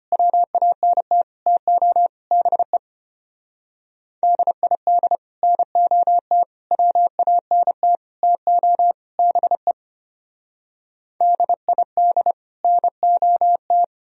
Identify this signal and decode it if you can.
Morse code, heard as a keyed tone, about 32 wpm, 700 Hz. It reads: WANT TO BE DID NOT WANT TO BE DID NOT